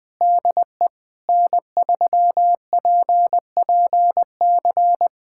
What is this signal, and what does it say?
Morse code, heard as a keyed tone, about 19 wpm, 700 Hz. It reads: DE N3PPC
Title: Practice QSO - Exchange 4 - Ending - 20wpm